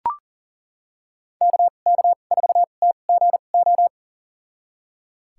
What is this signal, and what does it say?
Morse code, heard as a keyed tone, about 40 wpm, spaced as for 29 wpm, 700 Hz. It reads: KK4TGO
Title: USA Call Signs - 6 Characters - 2x Character Spacing - 40wpm